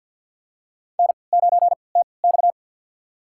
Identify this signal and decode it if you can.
Morse code, heard as a keyed tone, about 53 wpm, spaced as for 29 wpm, 700 Hz. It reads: N9TX
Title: USA Call Signs - 4 Characters - 3x Character Spacing - 50wpm